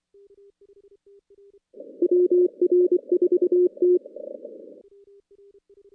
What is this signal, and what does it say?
Morse code, heard as a keyed tone, about 24 wpm, 360 Hz. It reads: WR4T
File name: contests / CW-SS-k5tr- -2100